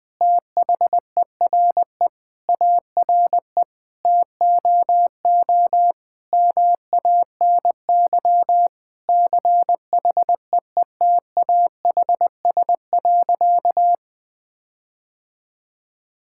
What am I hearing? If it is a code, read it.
Morse code, heard as a keyed tone, about 21 wpm, 700 Hz. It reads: THERE ARE TOO MANY CHEETAHS.
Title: All-Age Appropriate Jokes - 20wpm